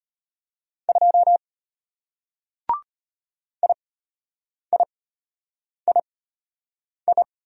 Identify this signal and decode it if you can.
Morse code, heard as a keyed tone, about 32 wpm, spaced as for 8 wpm, 690 Hz. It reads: 2 IIII